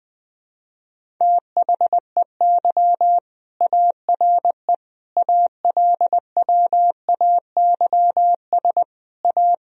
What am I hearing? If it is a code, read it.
Morse code, heard as a keyed tone, about 20 wpm, 700 Hz. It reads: THEY ARE ALWAYS A
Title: Sentences from Top 1000 Words - Review Part 4 - 20wpm